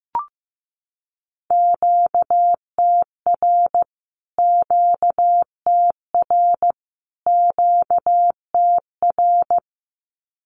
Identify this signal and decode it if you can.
Morse code, heard as a keyed tone, about 15 wpm, 700 Hz. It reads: QTR QTR QTR
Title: Q-Codes Repeated 3x 15wpm